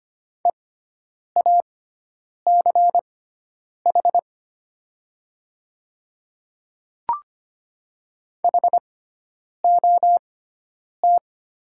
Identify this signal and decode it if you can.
Morse code, heard as a keyed tone, about 25 wpm, spaced as for 9 wpm, 700 Hz. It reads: EACH HOT
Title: 6x Character Spacing - Top 100 Words - 25wpm